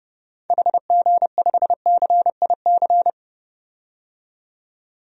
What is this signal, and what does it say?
Morse code, heard as a keyed tone, about 30 wpm, 700 Hz. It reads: HG5CIC
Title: DXCC-100 Call Signs - 6 Characters - 30wpm